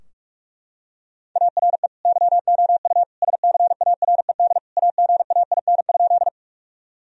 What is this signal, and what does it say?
Morse code, heard as a keyed tone, about 44 wpm, 700 Hz. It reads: ARE YOU SCARED AGAIN?
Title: Sentences from Top 1000 Words - Review Part 4 - 45wpm